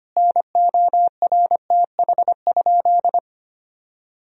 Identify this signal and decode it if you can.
Morse code, heard as a keyed tone, about 25 wpm, 700 Hz. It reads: NORTH?
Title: Sentences from Top 1000 Words - Review Part 2 - 25wpm